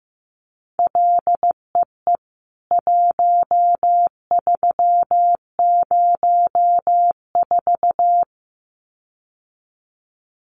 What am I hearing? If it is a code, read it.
Morse code, heard as a keyed tone, about 15 wpm, 700 Hz. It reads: LEE 1304
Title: CWT Exchange 15wpm